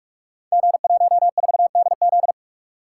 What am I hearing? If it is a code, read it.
Morse code, heard as a keyed tone, about 46 wpm, 700 Hz. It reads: G14DZ
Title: Code Groups - 5 Letter:Numbers - 45wpm